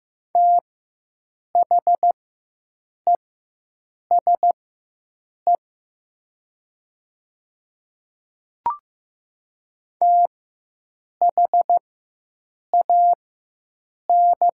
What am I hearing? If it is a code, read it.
Morse code, heard as a keyed tone, about 15 wpm, spaced as for 7 wpm, 700 Hz. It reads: THESE THAN
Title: 4x Character Spacing - Top 200 Words - 15wpm